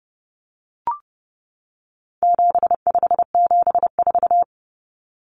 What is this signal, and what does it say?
Morse code, heard as a keyed tone, about 30 wpm, 700 Hz. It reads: 7574
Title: Common Cut Numbers - 4-Digit 30wpm